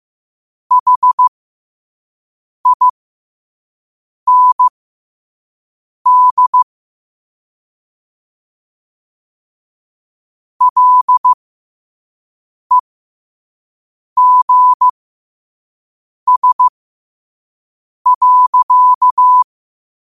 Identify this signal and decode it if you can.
Morse code, heard as a keyed tone, about 15 wpm, spaced as for 5 wpm, 1000 Hz. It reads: HIND LEGS.
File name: qotdfiles / 5wpm